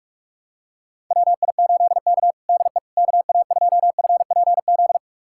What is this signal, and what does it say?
Morse code, heard as a keyed tone, about 41 wpm, 690 Hz. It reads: WI8K DE KA2FPZ
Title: USA Callsign DE Callsign 45wpm